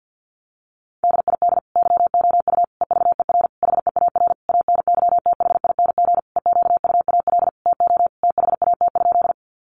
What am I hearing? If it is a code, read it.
Morse code, heard as a keyed tone, about 48 wpm, 700 Hz. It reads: DID YOU EVER HEAR ANYTHING EQUAL TO THAT?
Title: Sentences from Top 1400 Words 50wpm